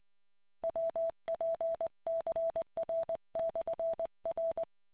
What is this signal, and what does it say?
Morse code, heard as a keyed tone, about 20 wpm, 660 Hz. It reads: WPCR/R